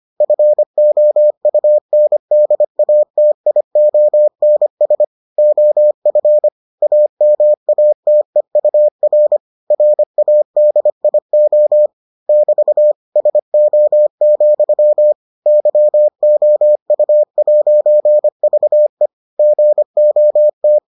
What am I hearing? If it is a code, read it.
Morse code, heard as a keyed tone, about 25 wpm, 600 Hz. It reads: FOUNDATIONS OF AMATEUR RADIO <BT> SO, YOU'VE GOT